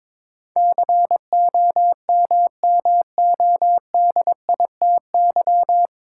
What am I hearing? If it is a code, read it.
Morse code, heard as a keyed tone, about 22 wpm, 700 Hz. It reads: COMMODITY
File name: 9 Letter Words - First 3 Letters Given - 22wpm